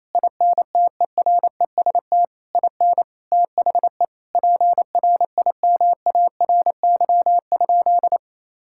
Morse code, heard as a keyed tone, about 28 wpm, 700 Hz. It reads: INTEREST IN THE PRIMARY?
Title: Sentences from Top 1000 Words - Review Part 5 - 28wpm